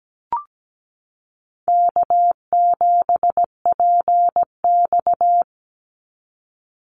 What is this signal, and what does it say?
Morse code, heard as a keyed tone, about 17 wpm, 700 Hz. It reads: K7PX
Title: NA SOTA Activator Callsigns 17wpm